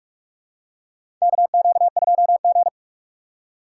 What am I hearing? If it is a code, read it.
Morse code, heard as a keyed tone, about 44 wpm, 700 Hz. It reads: KQ2G
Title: USA Call Signs - 4 Characters - 3x Character Spacing - 45wpm